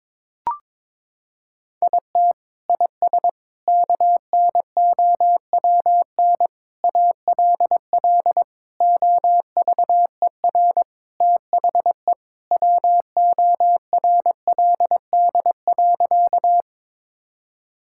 Morse code, heard as a keyed tone, about 21 wpm, 700 Hz. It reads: IT IS KNOWN ALL OVER THE WORLD.
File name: practice / 22wpm / Sentences with 4-Word Phrases - 22wpm